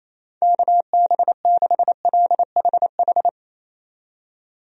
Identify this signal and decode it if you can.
Morse code, heard as a keyed tone, about 27 wpm, 700 Hz. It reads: KB6LHH
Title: USA Call Signs - 6 Characters - 28wpm